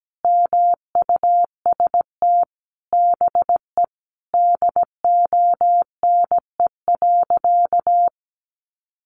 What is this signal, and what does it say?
Morse code, heard as a keyed tone, about 17 wpm, 700 Hz. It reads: MUST BE DONE.
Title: Sentences from Top 1100 Words 17wpm